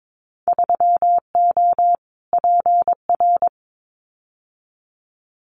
Morse code, heard as a keyed tone, about 22 wpm, 700 Hz. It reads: 3O PR
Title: Winter Field Day Exchanges Encore 22wpm